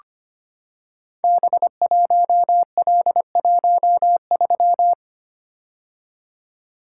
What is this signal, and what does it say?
Morse code, heard as a keyed tone, about 25 wpm, 700 Hz. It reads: B1L13
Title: Code Groups - 5 Letter:Numbers - 25wpm